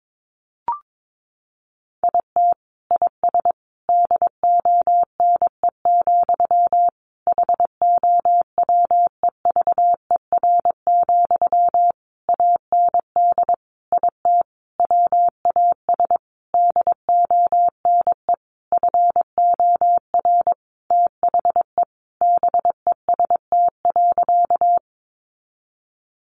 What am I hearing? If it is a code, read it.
Morse code, heard as a keyed tone, about 22 wpm, 700 Hz. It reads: IT IS DONE, HOWEVER, AND IT WAS DONE FOR THE BEST.
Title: Sentences from Top 600 Words 22wpm